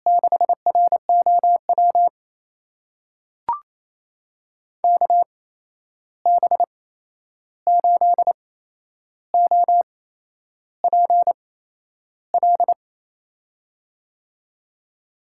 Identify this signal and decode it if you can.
Morse code, heard as a keyed tone, about 27 wpm, spaced as for 10 wpm, 700 Hz. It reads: 6ROW KB8OPL
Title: USA Call Signs - 6 Characters - 8x Character Spacing - 28wpm